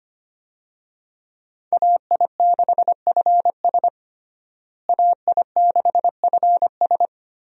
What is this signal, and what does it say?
Morse code, heard as a keyed tone, about 25 wpm, 700 Hz. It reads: AI6FS AI6FS
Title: USA Call Signs - 5 Characters - Repeated 3x - 25wpm